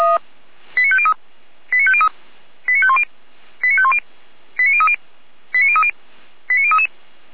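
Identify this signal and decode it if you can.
Five-tone selective call (ZVEI), heard as quick runs of five tones, tones of about 70 ms. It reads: 89503 89503 89510 89510 89030 89030 89033
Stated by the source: ZVEI five-tone selective call received by SDR.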